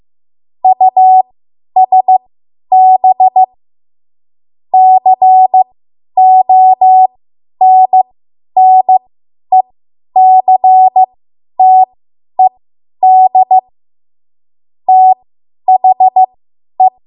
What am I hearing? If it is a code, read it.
Morse code, heard as a keyed tone, about 15 wpm, spaced as for 10 wpm, 750 Hz. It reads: USB CONNECTED THE
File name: Morse / 10WPM